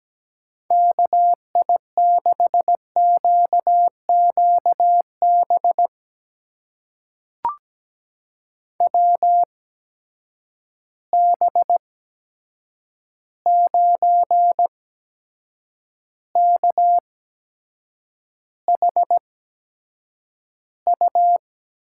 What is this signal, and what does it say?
Morse code, heard as a keyed tone, about 17 wpm, spaced as for 7 wpm, 700 Hz. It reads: KI6QQB WB9KHU
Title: USA Call Signs - 6 Characters - 8x Character Spacing - 17wpm